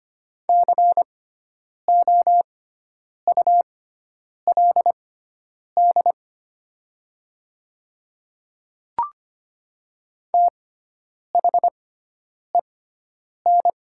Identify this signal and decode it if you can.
Morse code, heard as a keyed tone, about 25 wpm, spaced as for 9 wpm, 700 Hz. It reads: COULD THEN
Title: 6x Character Spacing - Top 100 Words - 25wpm